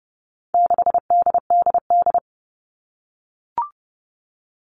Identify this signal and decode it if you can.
Morse code, heard as a keyed tone, about 30 wpm, 700 Hz. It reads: 6DDD
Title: All Cut Numbers - 4-Digit 30wpm